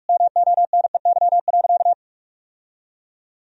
Morse code, heard as a keyed tone, about 44 wpm, 700 Hz. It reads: MONEY.
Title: Sentences from Top 1000 Words - Review Part 1 - 45wpm